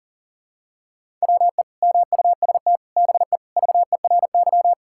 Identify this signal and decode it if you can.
Morse code, heard as a keyed tone, about 40 wpm, 700 Hz. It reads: WE MUST BE VERY
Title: Sentences from Top 1600 Words 40wpm